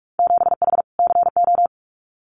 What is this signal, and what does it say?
Morse code, heard as a keyed tone, about 44 wpm, 700 Hz. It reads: 7H CO
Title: Winter Field Day Exchanges Encore 45wpm